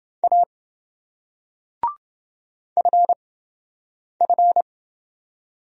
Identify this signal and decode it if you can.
Morse code, heard as a keyed tone, about 28 wpm, spaced as for 7 wpm, 700 Hz. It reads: A FF